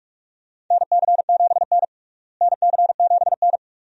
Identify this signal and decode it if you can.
Morse code, heard as a keyed tone, about 46 wpm, 700 Hz. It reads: NC7N NC7N
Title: USA Call Signs - 4 Characters - Repeated 3x - 45wpm